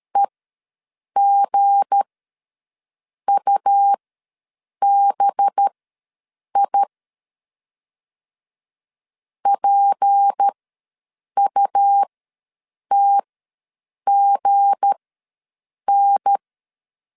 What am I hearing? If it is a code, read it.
Morse code, heard as a keyed tone, about 13 wpm, spaced as for 7 wpm, 800 Hz. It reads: EGUBI PUTGN
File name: LEZIONI / 8 YJ45 / 8 Y ws 12wpm 3409